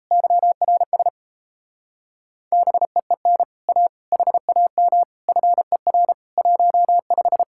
Morse code, heard as a keyed tone, about 38 wpm, 700 Hz. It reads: YRS BEEN A HAM FER 15